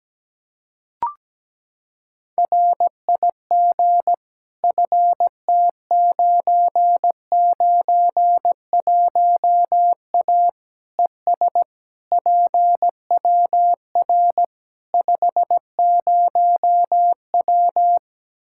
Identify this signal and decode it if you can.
Morse code, heard as a keyed tone, about 17 wpm, 700 Hz. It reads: RIG FT991A ES PWR 50W
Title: K1USN SST QSO Practice 17wpm